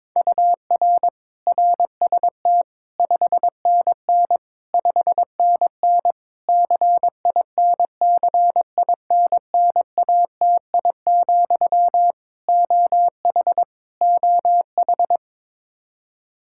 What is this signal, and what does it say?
Morse code, heard as a keyed tone, about 22 wpm, 700 Hz. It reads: UR RST 5NN 5NN CINCINNATI, OH OH
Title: Practice QSO - Exchange 1 - Beginning - 22wpm